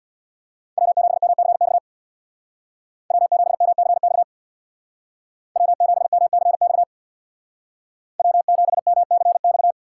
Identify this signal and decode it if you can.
Morse code, heard as a keyed tone, about 65 wpm, 700 Hz. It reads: W7MCX W7MCX W7MCX W7MCX